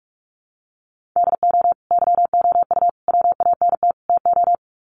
Morse code, heard as a keyed tone, about 44 wpm, 700 Hz. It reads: DO YOU WANT TO
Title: Sets of 4 Words 45wpm